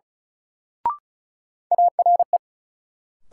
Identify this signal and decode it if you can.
Morse code, heard as a keyed tone, about 36 wpm, 700 Hz. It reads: ARE
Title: Sentences from Top 100 Words - New Word Drill - 35wpm